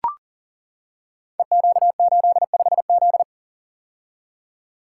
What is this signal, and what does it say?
Morse code, heard as a keyed tone, about 40 wpm, 700 Hz. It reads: EQ85Z
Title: Code Groups - 5 Letter:Numbers - 40wpm